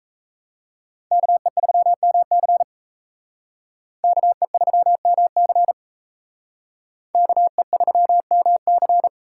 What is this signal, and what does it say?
Morse code, heard as a keyed tone, about 37 wpm, 700 Hz. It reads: KE3MC KE3MC KE3MC